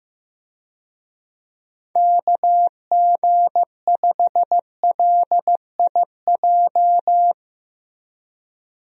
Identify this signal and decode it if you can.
Morse code, heard as a keyed tone, about 15 wpm, 700 Hz. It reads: KG5LIJ
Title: USA Call Signs - 6 Characters - Repeated 3x - 15wpm